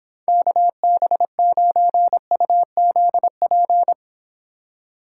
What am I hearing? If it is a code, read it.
Morse code, heard as a keyed tone, about 26 wpm, 700 Hz. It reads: KB9UZP